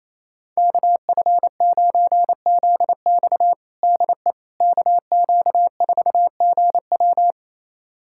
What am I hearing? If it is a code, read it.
Morse code, heard as a keyed tone, about 28 wpm, 700 Hz. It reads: KF9ZX DE KQ4GW
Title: USA Callsign DE Callsign - 5 Characters - 28wpm